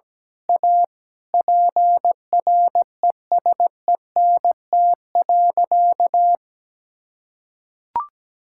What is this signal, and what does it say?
Morse code, heard as a keyed tone, about 17 wpm, 700 Hz. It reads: A PRESENT.
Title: Sentences from Top 1000 Words - Review Part 5 - 17wpm